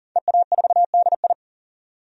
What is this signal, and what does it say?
Morse code, heard as a keyed tone, about 40 wpm, 700 Hz. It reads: EA4DI
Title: DXCC-100 Call Signs - 5 Characters - 40wpm